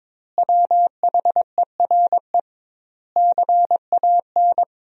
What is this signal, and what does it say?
Morse code, heard as a keyed tone, about 21 wpm, 700 Hz. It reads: WHERE CAN
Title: 2x Word-Spacing - Sentences from Top 500 Words - 22wpm